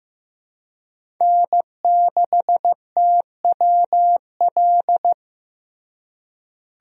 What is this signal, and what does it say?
Morse code, heard as a keyed tone, about 15 wpm, 700 Hz. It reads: N6TWL
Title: USA Call Signs - 5 Characters - Repeated 3x - 15wpm